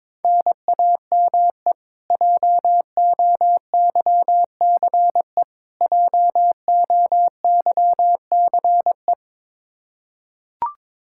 Morse code, heard as a keyed tone, about 22 wpm, 700 Hz. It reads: NAME JOYCE JOYCE
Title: Practice QSO - Exchange 1 - Beginning - 22wpm